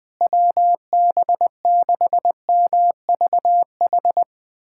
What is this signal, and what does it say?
Morse code, heard as a keyed tone, about 20 wpm, 700 Hz. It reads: WB6MVH
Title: USA Callsign DE Callsign - 6 Characters - 20wpm